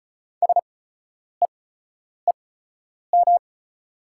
Morse code, heard as a keyed tone, about 34 wpm, spaced as for 10 wpm, 700 Hz. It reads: SEEM
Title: 8x Character Spacing - Top 300 Words - 35wpm